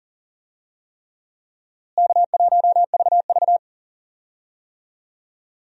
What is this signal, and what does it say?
Morse code, heard as a keyed tone, about 40 wpm, 700 Hz. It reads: K1VV
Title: USA Call Signs - 4 Characters - Repeated 3x - 40wpm